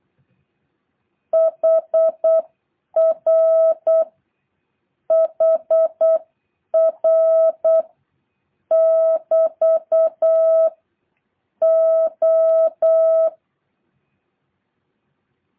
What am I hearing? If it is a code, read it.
Morse code, heard as a keyed tone, about 8 wpm, 650 Hz. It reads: HR HR <BT> O
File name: jh8cbh